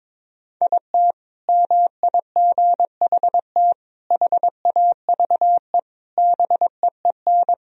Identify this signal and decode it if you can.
Morse code, heard as a keyed tone, about 22 wpm, 700 Hz. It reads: IT MIGHT HAVE BEEN
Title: Sentences from Top 1000 Words - Review Part 3 - 22wpm